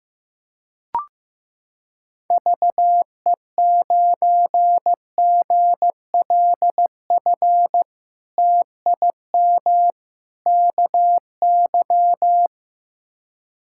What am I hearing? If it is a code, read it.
Morse code, heard as a keyed tone, about 15 wpm, 700 Hz. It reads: VE9GLF TIM KY
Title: K1USN SST QSO Practice - 15wpm